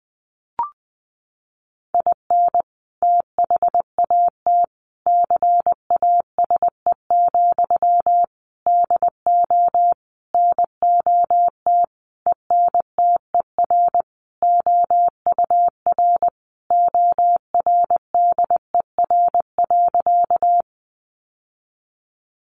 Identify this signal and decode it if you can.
Morse code, heard as a keyed tone, about 20 wpm, 700 Hz. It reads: IN THAT CASE, DO NOT ENTER OUR ORDER.
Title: Sentences from Top 1300 Words 20wpm